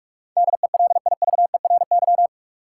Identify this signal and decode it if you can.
Morse code, heard as a keyed tone, about 46 wpm, 700 Hz. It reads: DELIVERY
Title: Sentences from Top 1500 Words - New Word Drill - 45wpm